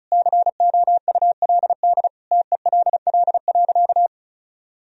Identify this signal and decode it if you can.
Morse code, heard as a keyed tone, about 34 wpm, 700 Hz. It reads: COULD TELL.
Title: Sentences from Top 1000 Words - Review Part 4 - 35wpm